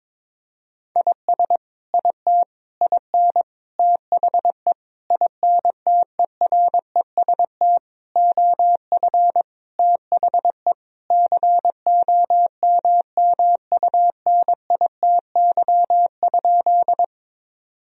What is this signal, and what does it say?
Morse code, heard as a keyed tone, about 22 wpm, 700 Hz. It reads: IS IT IN THE INTEREST OF THE COMMUNITY?
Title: Sentences from Top 1000 Words - Review Part 4 - 22wpm